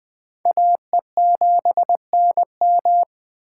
Morse code, heard as a keyed tone, about 20 wpm, 700 Hz. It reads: AE7NM